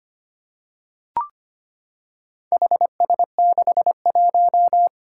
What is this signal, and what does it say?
Morse code, heard as a keyed tone, about 25 wpm, 700 Hz. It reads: HS61
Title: Four Challenging Letter-Number Combinations 25wpm